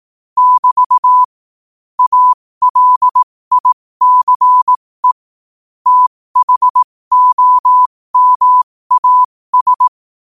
Morse code, heard as a keyed tone, about 18 wpm, spaced as for 15 wpm, 1000 Hz. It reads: <BT> ALICE THOMAS
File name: qotdfiles / 15wpm